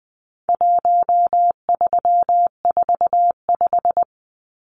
Morse code, heard as a keyed tone, about 20 wpm, 700 Hz. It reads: 1345
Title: Numbers - 4-Digit 20wpm